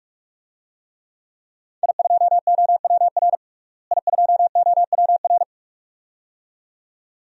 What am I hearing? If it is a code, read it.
Morse code, heard as a keyed tone, about 46 wpm, 700 Hz. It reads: I2OWR I2OWR
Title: DXCC-100 Call Signs - 5 Characters - Repeated 2x - 45wpm